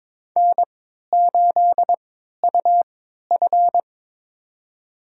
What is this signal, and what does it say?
Morse code, heard as a keyed tone, about 21 wpm, spaced as for 12 wpm, 700 Hz. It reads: N8UF
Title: USA Call Signs - 3x Character Spacing - 22wpm